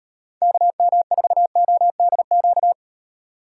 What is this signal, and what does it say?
Morse code, heard as a keyed tone, about 38 wpm, 700 Hz. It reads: KM4ODQ